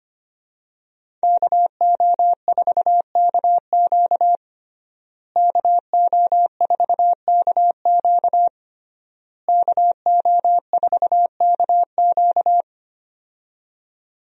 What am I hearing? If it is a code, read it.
Morse code, heard as a keyed tone, about 25 wpm, 700 Hz. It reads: KO4KQ KO4KQ KO4KQ
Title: USA Call Signs - 5 Characters - Repeated 3x - 25wpm